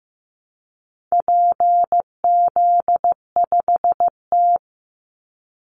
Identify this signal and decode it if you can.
Morse code, heard as a keyed tone, about 14 wpm, 700 Hz. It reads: PZ5T
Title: Call Signs Repeated 2x - 15wpm